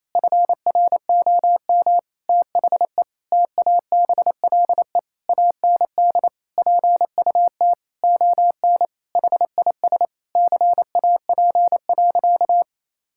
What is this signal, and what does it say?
Morse code, heard as a keyed tone, about 28 wpm, 700 Hz. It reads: FROM THE TABLE AND PUT ON HIS CAP.
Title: Sentences from Top 1600 Words 28wpm